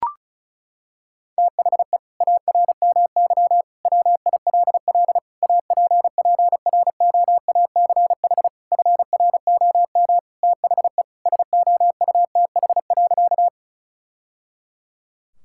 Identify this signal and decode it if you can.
Morse code, heard as a keyed tone, about 36 wpm, 690 Hz. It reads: THE ARMY WILL APPROACH FROM THE SOUTH.
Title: Sentences from Top 1000 Words - Review Part 6 - 35wpm